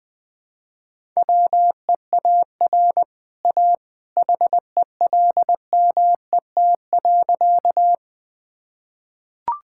Morse code, heard as a keyed tone, about 20 wpm, 700 Hz. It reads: WEAR A HELMET.
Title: Easy Sentences with Hints - 20wpm